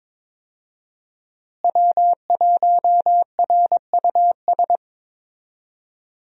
Morse code, heard as a keyed tone, about 21 wpm, 700 Hz. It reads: W1RUS
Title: USA Call Signs - 5 Characters - Repeated 2x - 22wpm